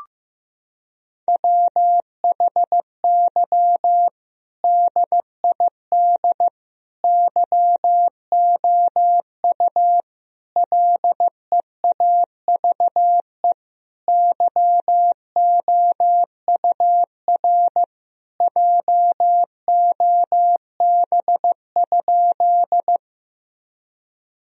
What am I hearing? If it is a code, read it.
Morse code, heard as a keyed tone, about 15 wpm, 700 Hz. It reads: WHY DID YOU LEAVE YOUR JOB?
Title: Sentences from Top 1000 Words - Review Part 3 - 15wpm